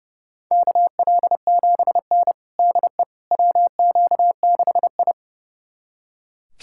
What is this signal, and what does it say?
Morse code, heard as a keyed tone, about 30 wpm, 700 Hz. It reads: KL7N DE WQ6I
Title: USA Callsign DE Callsign - 4 Characters - 30wpm